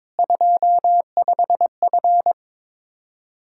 Morse code, heard as a keyed tone, about 21 wpm, 700 Hz. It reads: 25F
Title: Practice QSO - Exchange 2 - Info - 22wpm